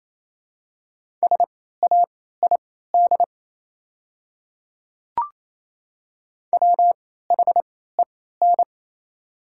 Morse code, heard as a keyed tone, about 28 wpm, spaced as for 16 wpm, 700 Hz. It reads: SAID WHEN
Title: 3x Character Spacing - Top 100 Words - 28wpm